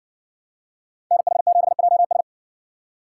Morse code, heard as a keyed tone, about 55 wpm, 700 Hz. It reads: NH7JS